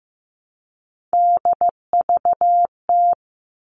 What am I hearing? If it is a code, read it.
Morse code, heard as a keyed tone, about 15 wpm, 700 Hz. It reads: DVT
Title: All Cut Numbers - 3-Digit 15wpm